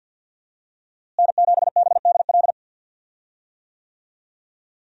Morse code, heard as a keyed tone, about 52 wpm, 700 Hz. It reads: N7BDL
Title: USA Call Signs - 5 Characters - Repeated 3x - 50wpm